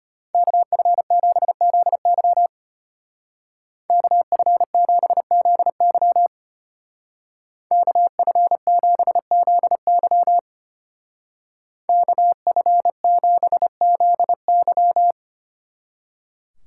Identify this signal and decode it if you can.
Morse code, heard as a keyed tone, about 30 wpm, 700 Hz. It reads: KF7ZY KF7ZY KF7ZY KF7ZY
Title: Speed Racing - USA 2x2 Call Signs 25wpm